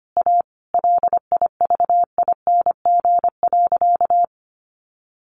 Morse code, heard as a keyed tone, about 25 wpm, 700 Hz. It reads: A LIVING.
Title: Sentences from Top 1500 Words 25wpm